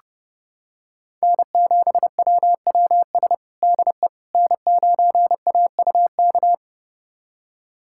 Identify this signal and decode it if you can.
Morse code, heard as a keyed tone, about 30 wpm, 700 Hz. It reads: N7WWS DE N9AUK
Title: USA Callsign DE Callsign - 5 Characters - 30wpm